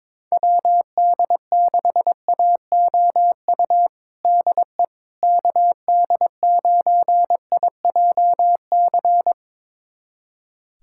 Morse code, heard as a keyed tone, about 22 wpm, 700 Hz. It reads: WD6AOU DE KD9IJC
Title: USA Callsign DE Callsign - 6 Characters - 22wpm